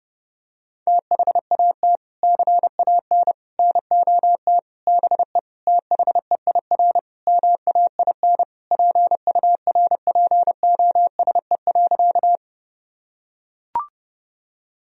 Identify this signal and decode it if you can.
Morse code, heard as a keyed tone, about 30 wpm, 700 Hz. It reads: THAT CAN NOT BE THEIR MAIN PURPOSE.
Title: Sentences from Top 1000 Words - Review Part 2 - 30wpm